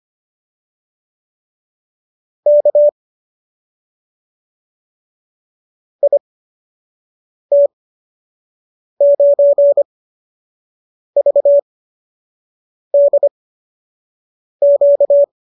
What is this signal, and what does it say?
Morse code, heard as a keyed tone, about 25 wpm, spaced as for 6 wpm, 570 Hz. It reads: K IT9VDQ